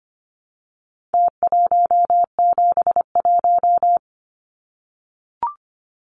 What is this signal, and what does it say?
Morse code, heard as a keyed tone, about 24 wpm, 700 Hz. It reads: T171
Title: Numbers - Contest Style - 4-Digit 25wpm